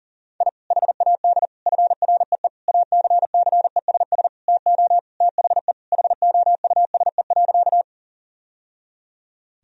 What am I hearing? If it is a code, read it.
Morse code, heard as a keyed tone, about 40 wpm, 700 Hz. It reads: I HAD FREE ACCESS TO THE HOUSE.
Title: Sentences from Top 1000 Words - Review Part 4 - 40wpm